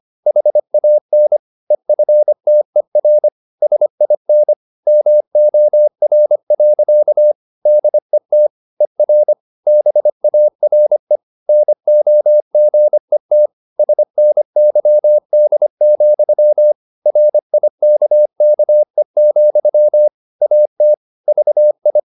Morse code, heard as a keyed tone, about 25 wpm, 600 Hz. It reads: HAN EFTER SIN MOR. DET ER BARE NOGET SNYD, RIKKE, AT VI
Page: Ferie 25 wpm